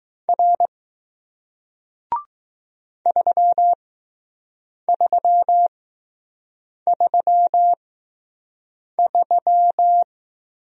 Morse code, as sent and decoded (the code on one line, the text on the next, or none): .-. / ...-- ...-- ...-- ...--
R 3333